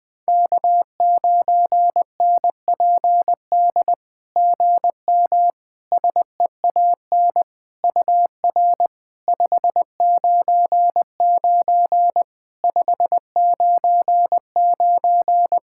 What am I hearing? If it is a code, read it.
Morse code, heard as a keyed tone, about 20 wpm, 700 Hz. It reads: K9NPD GM SEAN UR 599 599
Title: SOTA QSO Practice 20wpm